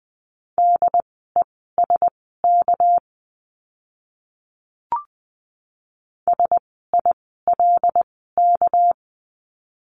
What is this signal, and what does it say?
Morse code, heard as a keyed tone, about 20 wpm, spaced as for 15 wpm, 700 Hz. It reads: DESK SILK
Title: 199 Lesson 32f - Letter k - Words 20wpm